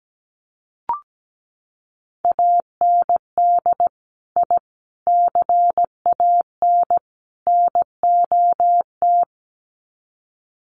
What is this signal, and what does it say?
Morse code, heard as a keyed tone, about 17 wpm, 700 Hz. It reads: AND I CAN NOT
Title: Sets of 4 Words 17wpm